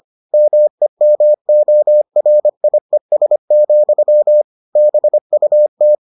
Morse code, heard as a keyed tone, about 25 wpm, 600 Hz. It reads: MEMORIES, BUT